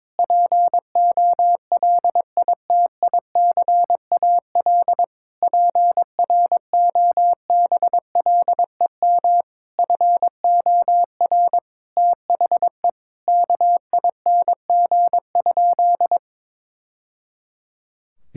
Morse code, heard as a keyed tone, about 22 wpm, 700 Hz. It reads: POLITICAL PROBLEM FOR THE KING?
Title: Sentences from Top 1000 Words - Review Part 5 - 22wpm